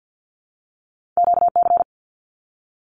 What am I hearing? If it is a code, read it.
Morse code, heard as a keyed tone, about 50 wpm, 700 Hz. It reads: QC
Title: Canadian Provinces and Territories 50wpm